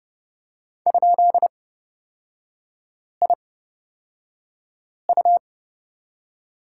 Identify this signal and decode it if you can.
Morse code, heard as a keyed tone, about 30 wpm, spaced as for 5 wpm, 700 Hz. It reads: ?IU
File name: Lesson 37e - Letter x - Three Characters 30wpm